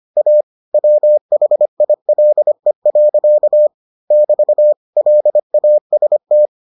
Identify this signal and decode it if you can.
Morse code, heard as a keyed tone, about 25 wpm, 600 Hz. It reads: A WHILE. <BT> LAST